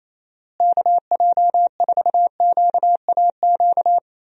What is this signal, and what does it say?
Morse code, heard as a keyed tone, about 27 wpm, 700 Hz. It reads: KJ4QAQ
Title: USA Call Signs - 6 Characters - Repeated 2x - 28wpm